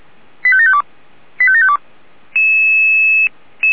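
Five-tone selective call (ZVEI), heard as quick runs of five tones, tones of about 80 ms. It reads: 85752 85752
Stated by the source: Five-tone ZVEI selective call received by SDR, from Feuerwehr Orlamünde.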